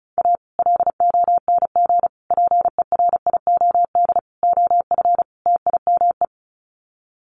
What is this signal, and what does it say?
Morse code, heard as a keyed tone, about 34 wpm, 700 Hz. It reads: A LONG PERIOD OF TIME
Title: Sets of 5 Words 35wpm